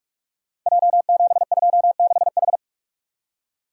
Morse code, heard as a keyed tone, about 44 wpm, 690 Hz. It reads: J726H
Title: Code Groups - 5 Letter:Numbers - 45wpm